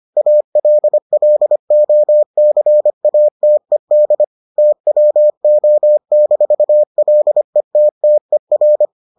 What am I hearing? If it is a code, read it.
Morse code, heard as a keyed tone, about 25 wpm, 600 Hz. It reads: ALLOCATED TWO-LETTER